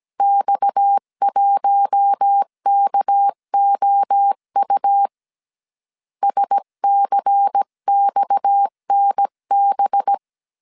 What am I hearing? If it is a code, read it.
Morse code, heard as a keyed tone, about 17 wpm, 800 Hz. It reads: X1KOU SCXNB
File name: CW / DET / 1 6 1500C S C 16wpm 2733